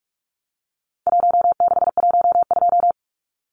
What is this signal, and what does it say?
Morse code, heard as a keyed tone, about 44 wpm, 700 Hz. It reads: 1612
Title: Numbers - Contest Style - 4-Digit 45wpm